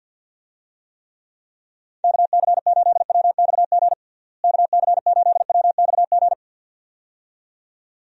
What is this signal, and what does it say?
Morse code, heard as a keyed tone, about 50 wpm, 700 Hz. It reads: KC8WXG KC8WXG
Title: USA Call Signs - 6 Characters - Repeated 2x - 50wpm